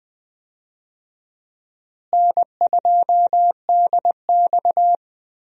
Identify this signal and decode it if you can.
Morse code, heard as a keyed tone, about 20 wpm, 700 Hz. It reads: N2DX
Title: USA Call Signs - 4 Characters - Repeated 3x - 20wpm